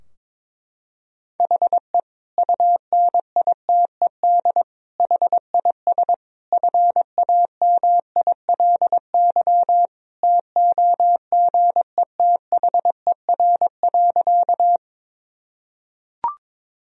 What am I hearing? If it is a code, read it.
Morse code, heard as a keyed tone, about 22 wpm, 700 Hz. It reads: HE UNITED HIS FAMILY TOGETHER.
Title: Sentences from Top 1000 Words - Review Part 4 - 22wpm